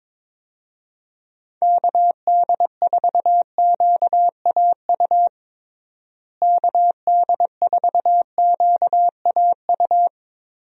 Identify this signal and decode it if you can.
Morse code, heard as a keyed tone, about 22 wpm, 700 Hz. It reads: KD4QAU KD4QAU
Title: USA Call Signs - 6 Characters - Repeated 3x - 22wpm